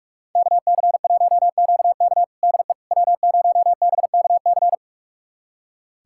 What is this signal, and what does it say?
Morse code, heard as a keyed tone, about 46 wpm, 700 Hz. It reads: KC1QK DE W0BKC
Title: USA Callsign DE Callsign - 5 Characters - 45wpm